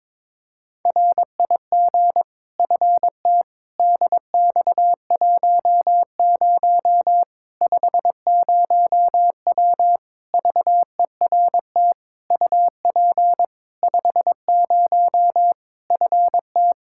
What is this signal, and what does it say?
Morse code, heard as a keyed tone, about 22 wpm, 700 Hz. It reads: RIG FT DX10 50W VERT UP 50 FT
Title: Practice QSO - Exchange 2 - Info - 22wpm